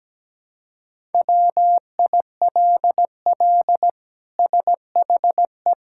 Morse code, heard as a keyed tone, about 17 wpm, 700 Hz. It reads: WILL SHE
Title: Sentences from Top 1000 Words - Review Part 5 - 17wpm